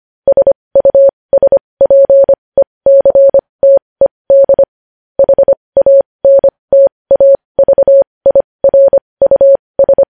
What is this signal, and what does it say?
Morse code, heard as a keyed tone, about 25 wpm, spaced as for 20 wpm, 550 Hz. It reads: SUSPECTED HANTAVIRUS
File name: podcast / News headlines-1x-25x20